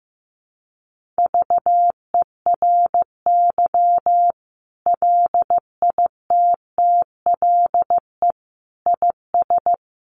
Morse code, heard as a keyed tone, about 15 wpm, 700 Hz. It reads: VERY LITTLE IS
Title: Sentences from Top 1500 Words 15wpm